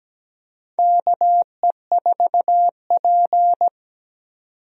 Morse code, heard as a keyed tone, about 17 wpm, 700 Hz. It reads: KE4P
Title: USA Call Signs - 4 Characters - 17wpm